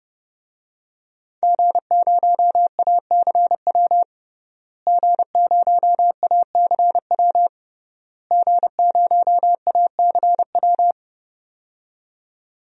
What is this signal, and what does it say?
Morse code, heard as a keyed tone, about 32 wpm, 700 Hz. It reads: G0ACW G0ACW G0ACW
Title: DXCC-100 Call Signs - 5 Characters - Repeated 3x - 30wpm